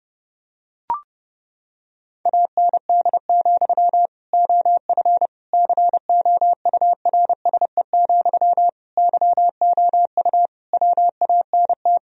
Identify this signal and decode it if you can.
Morse code, heard as a keyed tone, about 30 wpm, 700 Hz. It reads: AND, OF COURSE, YOU WANT
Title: Sentences from Top 1000 Words - Review Part 1 - 30wpm